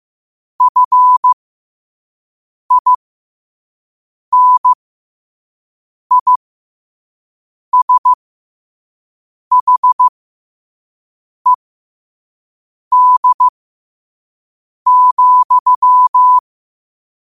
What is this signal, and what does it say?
Morse code, heard as a keyed tone, about 15 wpm, spaced as for 5 wpm, 1000 Hz. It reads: FINISHED,
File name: qotdfiles / 5wpm